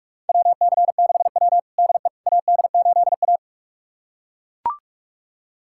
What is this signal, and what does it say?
Morse code, heard as a keyed tone, about 42 wpm, 700 Hz. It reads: WC6W DE AD8A
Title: USA Callsign DE Callsign - 4 Characters - 45wpm